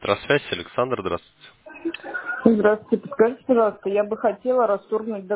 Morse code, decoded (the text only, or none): none